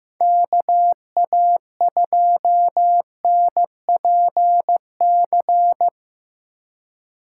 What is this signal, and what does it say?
Morse code, heard as a keyed tone, about 15 wpm, 700 Hz. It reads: KA2NPC
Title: USA Call Signs - 6 Characters - 4x Character Spacing - 15wpm